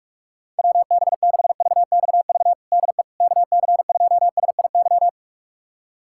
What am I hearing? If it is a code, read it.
Morse code, heard as a keyed tone, about 44 wpm, 700 Hz. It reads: WB6VXV DE KC2SIY
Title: USA Callsign DE Callsign 45wpm